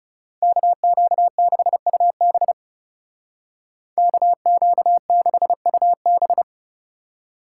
Morse code, heard as a keyed tone, about 36 wpm, 700 Hz. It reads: KQ6UB KQ6UB
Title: Speed Racing - USA 2x2 Call Signs 30wpm